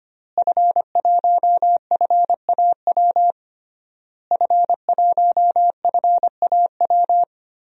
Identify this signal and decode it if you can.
Morse code, heard as a keyed tone, about 25 wpm, 700 Hz. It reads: F1FAW F1FAW
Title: DXCC-100 Call Signs - 5 Characters - Repeated 3x - 25wpm